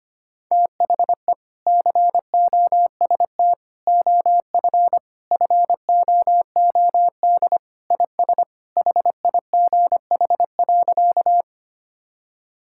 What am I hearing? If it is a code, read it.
Morse code, heard as a keyed tone, about 25 wpm, 700 Hz. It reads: THE COST OF FOOD IS HIGH.
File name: practice / 25wpm / Sentences with 3-Word Phrases - 25wpm